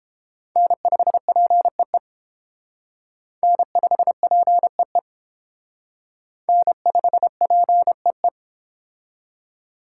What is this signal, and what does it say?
Morse code, heard as a keyed tone, about 29 wpm, 700 Hz. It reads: N5PEE N5PEE N5PEE